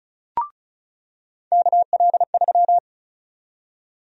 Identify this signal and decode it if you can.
Morse code, heard as a keyed tone, about 36 wpm, 700 Hz. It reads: KL3
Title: USA Call Sign Prefixes - 35wpm